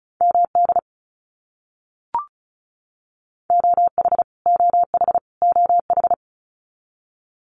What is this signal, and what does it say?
Morse code, heard as a keyed tone, about 34 wpm, 700 Hz. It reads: MD OH OH OH
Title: US States Repeated 3x 35wpm